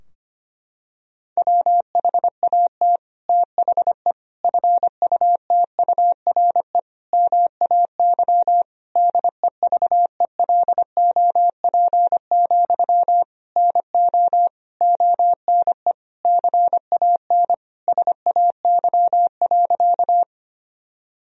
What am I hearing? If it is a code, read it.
Morse code, heard as a keyed tone, about 25 wpm, 700 Hz. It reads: WHAT THE FUTURE MAY DEVELOP, NO ONE CAN SAY.
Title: Sentences from Top 1000 Words - Review Part 2 - 25wpm